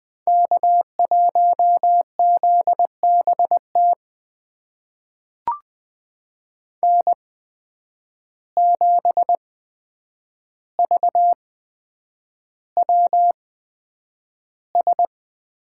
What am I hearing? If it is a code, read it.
Morse code, heard as a keyed tone, about 20 wpm, spaced as for 9 wpm, 700 Hz. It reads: K1ZBT N7VWS
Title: USA Call Signs - 5 Characters - 8x Character Spacing - 20wpm